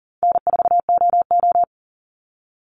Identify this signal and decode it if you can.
Morse code, heard as a keyed tone, about 40 wpm, 700 Hz. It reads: N4OO
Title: Call Signs Repeated 2x - 40wpm